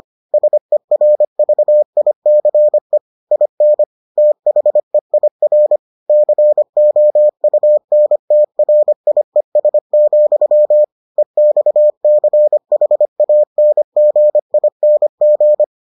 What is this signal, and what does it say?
Morse code, heard as a keyed tone, about 25 wpm, 600 Hz. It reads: SERVICE IN THEIR COUNTRIES, EXCHANGING